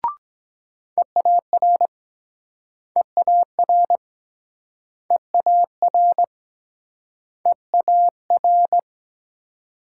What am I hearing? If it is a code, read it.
Morse code, heard as a keyed tone, about 23 wpm, 690 Hz. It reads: EAR EAR EAR EAR